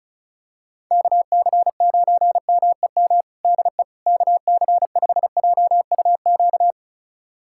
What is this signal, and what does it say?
Morse code, heard as a keyed tone, about 34 wpm, 700 Hz. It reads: KC9MEM DE KC5JUQ
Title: USA Callsign DE Callsign - 6 Characters - 35wpm